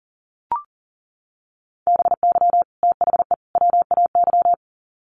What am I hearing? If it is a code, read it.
Morse code, heard as a keyed tone, about 40 wpm, 700 Hz. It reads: BY THE WAY
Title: Sets of 3 Words 40wpm